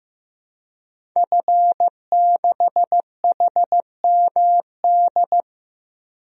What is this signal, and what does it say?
Morse code, heard as a keyed tone, about 15 wpm, 700 Hz. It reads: F6HMD
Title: DXCC-100 Call Signs - 5 Characters - 15wpm